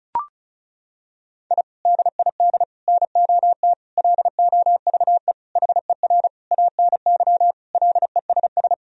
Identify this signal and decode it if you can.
Morse code, heard as a keyed tone, about 36 wpm, 700 Hz. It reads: I DID NOT LOVE HER ANY LESS
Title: Sentences from Top 1000 Words - Review Part 2 - 35wpm